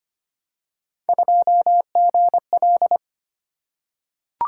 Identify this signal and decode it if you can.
Morse code, heard as a keyed tone, about 25 wpm, 700 Hz. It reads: 2GL
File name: Three Letter-Number Combinations - 25wpm